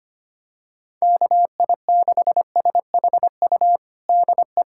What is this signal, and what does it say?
Morse code, heard as a keyed tone, about 25 wpm, 700 Hz. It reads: KI6SHU DE
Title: USA Callsign DE Callsign - 6 Characters - 25wpm